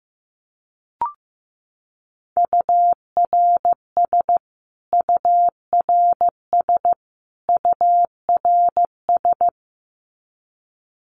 Morse code, heard as a keyed tone, about 15 wpm, 700 Hz. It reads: URS URS URS
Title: Abbreviations Repeated 3x - 15wpm